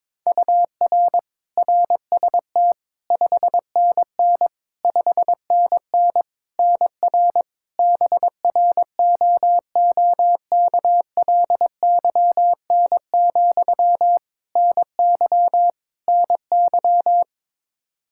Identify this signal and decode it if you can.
Morse code, heard as a keyed tone, about 22 wpm, 700 Hz. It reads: UR RST 5NN 5NN NR BROOKLYN, NY NY
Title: K1USN SST QSO Practice 22wpm